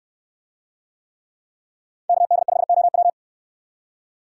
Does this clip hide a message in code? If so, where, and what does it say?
Yes: Morse code from 2.1 to 3.1 s, heard as a keyed tone, about 62 wpm, 700 Hz. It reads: KD5OW